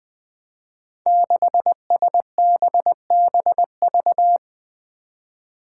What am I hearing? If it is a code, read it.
Morse code, heard as a keyed tone, about 20 wpm, 700 Hz. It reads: 6SBBV
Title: Five Challenging Letter-Number Combinations 20wpm